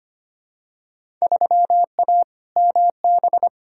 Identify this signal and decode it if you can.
Morse code, heard as a keyed tone, about 25 wpm, 700 Hz. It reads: 3A MB
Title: Field Day Exchanges Encore - 25wpm